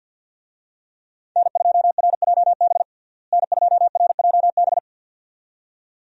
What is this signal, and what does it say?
Morse code, heard as a keyed tone, about 52 wpm, 700 Hz. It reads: N2RJB N2RJB
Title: USA Call Signs - 5 Characters - Repeated 2x - 50wpm